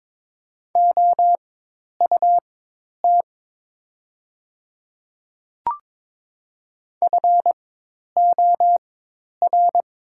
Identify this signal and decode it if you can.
Morse code, heard as a keyed tone, about 23 wpm, spaced as for 10 wpm, 700 Hz. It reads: OUT FOR
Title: 4x Character Spacing - Top 300 Words - 22wpm